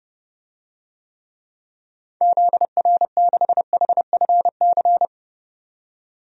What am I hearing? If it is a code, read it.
Morse code, heard as a keyed tone, about 31 wpm, 700 Hz. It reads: ZR6HFC
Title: DXCC-100 Call Signs - 6 Characters - 30wpm